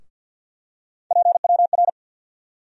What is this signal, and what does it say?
Morse code, heard as a keyed tone, about 53 wpm, 700 Hz. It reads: PWR
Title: USA Call Sign Postfixes - 3 Letters 50wpm